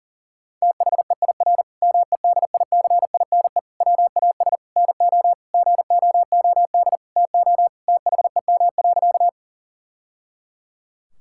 Morse code, heard as a keyed tone, about 40 wpm, 700 Hz. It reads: THEIR MEDICINE WAS NO GOOD TO THEM.
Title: Sentences from Top 1000 Words - Review Part 6 - 40wpm